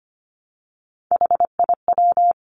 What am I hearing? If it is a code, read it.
Morse code, heard as a keyed tone, about 25 wpm, 700 Hz. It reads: HIW
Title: Three Challenging Letter-Number Combinations 25wpm